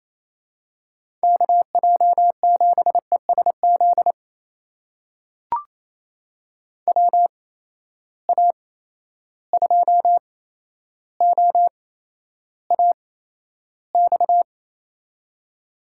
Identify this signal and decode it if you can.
Morse code, heard as a keyed tone, about 28 wpm, spaced as for 12 wpm, 700 Hz. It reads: KJ7ESZ WA2OAX
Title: USA Call Signs - 6 Characters - 8x Character Spacing - 28wpm